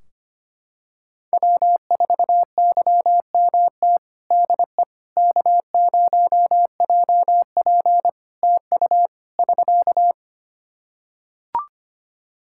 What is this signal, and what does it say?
Morse code, heard as a keyed tone, about 25 wpm, 700 Hz. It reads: W4YMT DE K0JP TU <SK>
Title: Practice QSO - Exchange 4 - Ending - 25wpm